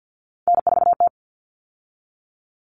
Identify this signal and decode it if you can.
Morse code, heard as a keyed tone, about 50 wpm, 700 Hz. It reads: N4T
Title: All Cut Numbers - 3-Digit 50wpm